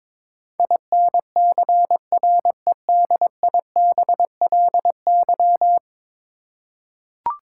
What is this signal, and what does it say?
Morse code, heard as a keyed tone, about 22 wpm, 700 Hz. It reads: INCREDIBLY
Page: morse-code-ninja-practice-22wpm